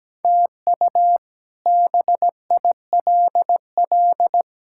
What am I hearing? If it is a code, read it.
Morse code, heard as a keyed tone, about 17 wpm, 700 Hz. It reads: TU BILL
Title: K1USN SST QSO Practice - 17wpm